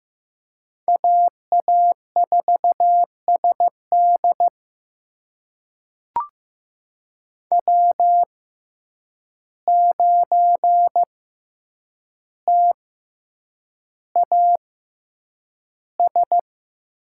Morse code, heard as a keyed tone, about 15 wpm, spaced as for 8 wpm, 700 Hz. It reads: AA4SD W9TAS
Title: USA Call Signs - 5 Characters - 6x Character Spacing - 15wpm